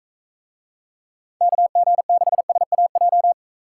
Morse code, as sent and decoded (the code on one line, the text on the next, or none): -.- --. -.... ... .- .---
KG6SAJ